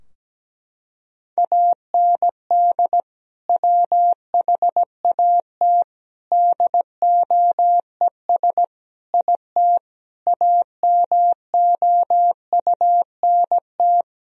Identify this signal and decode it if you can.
Morse code, heard as a keyed tone, about 17 wpm, 700 Hz. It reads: AND WHAT DOES IT AMOUNT
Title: Sentences from Top 1000 Words - Review Part 4 - 17wpm